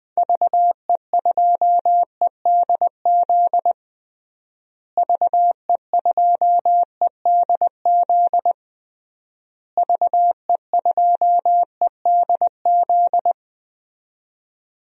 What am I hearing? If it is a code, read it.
Morse code, heard as a keyed tone, about 21 wpm, 700 Hz. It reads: VE2EDZ VE2EDZ VE2EDZ
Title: DXCC-100 Call Signs - 6 Characters - Repeated 3x - 20wpm